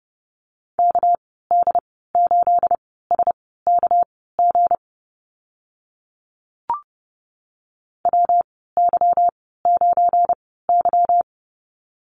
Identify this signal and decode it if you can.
Morse code, heard as a keyed tone, about 30 wpm, spaced as for 17 wpm, 700 Hz. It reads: KD8SKG WY9Y
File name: Lesson 37g - Letter x - Call Signs 30wpm